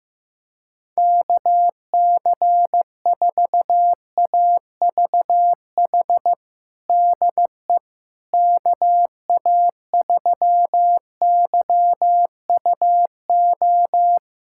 Morse code, heard as a keyed tone, about 15 wpm, 700 Hz. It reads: KC4AVH DE KA3YUO
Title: USA Callsign DE Callsign - 6 Characters - 15wpm